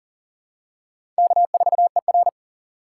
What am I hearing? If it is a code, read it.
Morse code, heard as a keyed tone, about 40 wpm, 700 Hz. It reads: K4ER
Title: USA Call Signs - 4 Characters - 8x Character Spacing - 40wpm